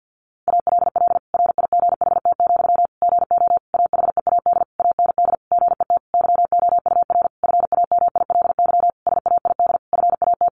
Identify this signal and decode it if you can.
Morse code, heard as a keyed tone, about 50 wpm, 700 Hz. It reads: ALL RIGHT, GO AHEAD AND GET YOUR FAMILY SAID FAT
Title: Sentences from Top 1100 Words 50wpm